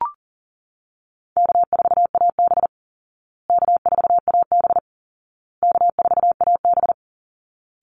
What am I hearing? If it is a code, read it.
Morse code, heard as a keyed tone, about 40 wpm, 700 Hz. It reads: K4AB K4AB K4AB
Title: Call Signs Repeated 2x - 40wpm